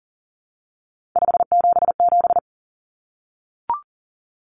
Morse code, heard as a keyed tone, about 40 wpm, 700 Hz. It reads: 577
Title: Numbers - 3-Digit 40wpm